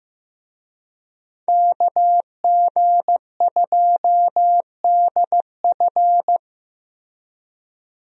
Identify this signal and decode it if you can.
Morse code, heard as a keyed tone, about 15 wpm, 700 Hz. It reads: KG2DF
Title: USA Call Signs - 5 Characters - Repeated 3x - 15wpm